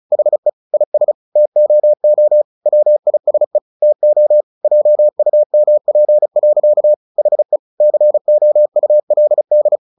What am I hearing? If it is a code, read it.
Morse code, heard as a keyed tone, about 34 wpm, 600 Hz. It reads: HE IS TOO WISE TO JUMP. HE COULD
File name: the old man and the sea / the old man and the sea 1 35wpm